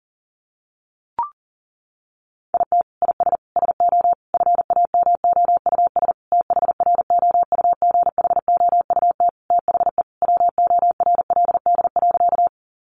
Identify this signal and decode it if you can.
Morse code, heard as a keyed tone, about 40 wpm, 700 Hz. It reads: IT IS SO FAMOUS THROUGHOUT THE WORLD.
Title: Sentences from Top 1500 Words 40wpm